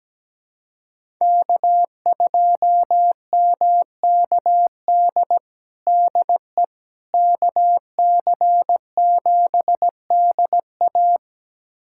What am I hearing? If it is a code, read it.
Morse code, heard as a keyed tone, about 17 wpm, 700 Hz. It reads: K2MKD DE KC7DA
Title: USA Callsign DE Callsign - 5 Characters - 17wpm